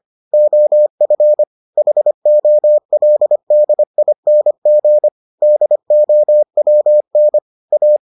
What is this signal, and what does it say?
Morse code, heard as a keyed tone, about 25 wpm, 600 Hz. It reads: OF HOLDING DOWN A